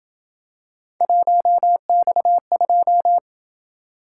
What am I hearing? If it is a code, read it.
Morse code, heard as a keyed tone, about 27 wpm, 700 Hz. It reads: 1X2